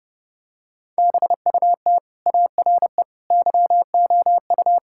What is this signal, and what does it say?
Morse code, heard as a keyed tone, about 30 wpm, 700 Hz. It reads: BUT ARE YOU
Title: Sentences from Top 1600 Words 30wpm